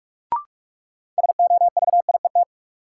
Morse code, heard as a keyed tone, about 46 wpm, 700 Hz. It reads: SOVIET
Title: Sets of 5 Words Encore - New Word Drill - 45wpm